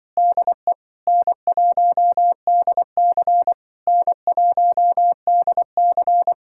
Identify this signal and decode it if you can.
Morse code, heard as a keyed tone, about 24 wpm, 700 Hz. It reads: DE N1DC N1DC
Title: Speed-Racing Common QSO Phrases Encore - 20wpm